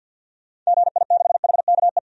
Morse code, heard as a keyed tone, about 51 wpm, 700 Hz. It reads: GI6HKE
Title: DXCC-100 Call Signs - 6 Characters - 50wpm